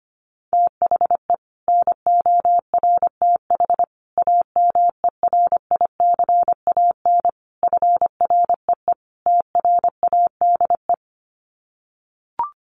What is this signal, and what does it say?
Morse code, heard as a keyed tone, about 25 wpm, 700 Hz. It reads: THE NORTH AMERICAN FREE TRADE
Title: Sets of 5 Words Encore 25wpm